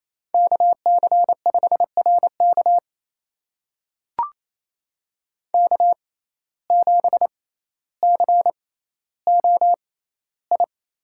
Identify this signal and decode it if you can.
Morse code, heard as a keyed tone, about 28 wpm, spaced as for 14 wpm, 700 Hz. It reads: KC5RK K7COI
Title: USA Call Signs - 5 Characters - 6x Character Spacing - 28wpm